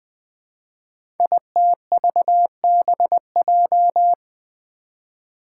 Morse code, heard as a keyed tone, about 20 wpm, 700 Hz. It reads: ITVBJ
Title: Code Groups - 5 Letters - 20wpm